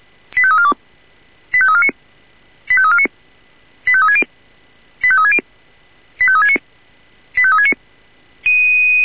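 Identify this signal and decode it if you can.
Five-tone selective call (ZVEI), heard as quick runs of five tones, tones of about 70 ms. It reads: 85342 85349 85349 85360 85360 85369 85369
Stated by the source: ZVEI five-tone selective call received on an SDR.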